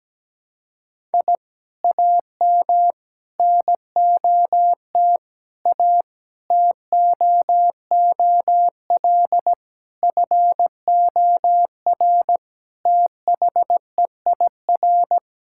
Morse code, heard as a keyed tone, about 17 wpm, 700 Hz. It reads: I AM NOT A TOOL FOR THEIR
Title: Sentences from Top 1000 Words - Review Part 6 - 17wpm